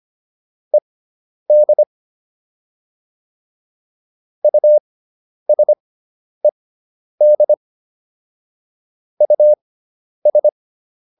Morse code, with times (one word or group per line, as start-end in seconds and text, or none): 0.7-1.8 ED
4.4-7.5 USED
9.2-10.5 US